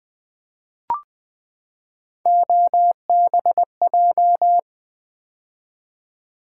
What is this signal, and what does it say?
Morse code, heard as a keyed tone, about 20 wpm, 700 Hz. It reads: OBJ
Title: Build A Word - 3 Letters At A Time - 9-Letter Words - 20wpm